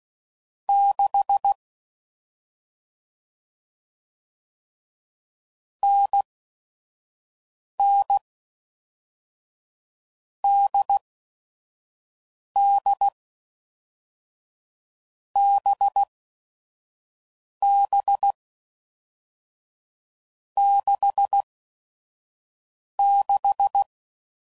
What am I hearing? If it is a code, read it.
Morse code, heard as a keyed tone, about 16 wpm, spaced as for 5 wpm, 800 Hz. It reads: 6 NN DD BB 66
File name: Mixed / Short Runs / NDB NDB6